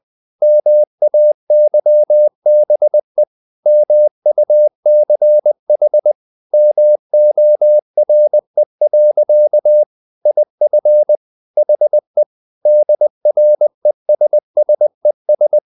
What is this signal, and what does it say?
Morse code, heard as a keyed tone, about 20 wpm, 600 Hz. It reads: MAYBE MUCH MORE. IF HE DRESSES
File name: the old man and the sea part 2 20wpm